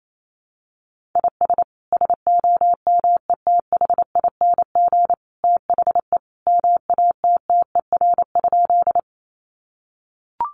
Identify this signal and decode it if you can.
Morse code, heard as a keyed tone, about 28 wpm, 700 Hz. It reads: IS SOMETHING THE MATTER?
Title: Sentences from Top 700 Words 28wpm